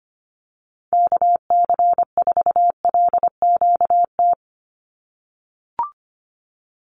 Morse code, heard as a keyed tone, about 25 wpm, 700 Hz. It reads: KC4LQT
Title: USA Call Signs 25wpm